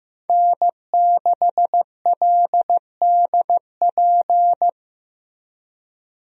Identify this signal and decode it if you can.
Morse code, heard as a keyed tone, about 15 wpm, 700 Hz. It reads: N6LDP
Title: USA Call Signs - 5 Characters - Repeated 2x - 15wpm